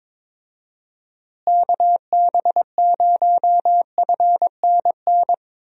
Morse code, heard as a keyed tone, about 22 wpm, 700 Hz. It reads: KB0FNN
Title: USA Call Signs - 6 Characters - Repeated 2x - 22wpm